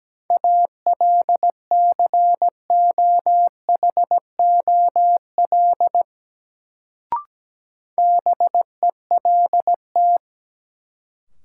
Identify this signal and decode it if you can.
Morse code, heard as a keyed tone, about 17 wpm, 700 Hz. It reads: ALCOHOL BELT
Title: Sentences from Top 1400 Words - New Word Drill Alt - 17wpm